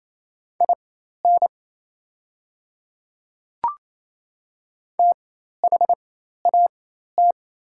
Morse code, heard as a keyed tone, about 28 wpm, spaced as for 13 wpm, 700 Hz. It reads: IN THAT
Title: 4x Character Spacing - Top 200 Words - 28wpm